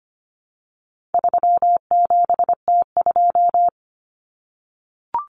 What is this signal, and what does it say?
Morse code, heard as a keyed tone, about 25 wpm, 700 Hz. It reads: 37T2
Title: Common Cut Numbers - 4-Digit 25wpm